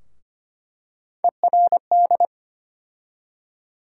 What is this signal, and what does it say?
Morse code, heard as a keyed tone, about 25 wpm, 700 Hz. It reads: ERD
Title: USA Call Sign Postfixes - 3 Letters 25wpm